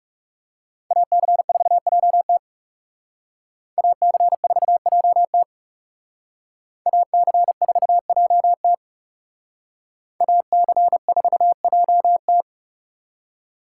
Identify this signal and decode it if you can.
Morse code, heard as a keyed tone, about 38 wpm, 700 Hz. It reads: AC4JT AC4JT AC4JT AC4JT